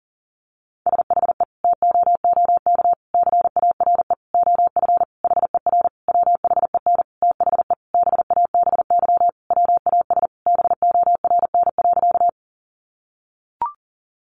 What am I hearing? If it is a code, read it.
Morse code, heard as a keyed tone, about 37 wpm, 700 Hz. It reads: SHE TOOK CARE OF HER WHEN THE BABY WAS BORN.
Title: Sentences from Top 1300 Words 40wpm